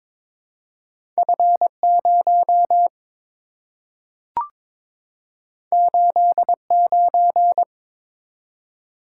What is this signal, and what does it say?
Morse code, heard as a keyed tone, about 22 wpm, 700 Hz. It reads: F0 89
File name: practice / 22wpm / Two Letter-Number Combinations - 22wpm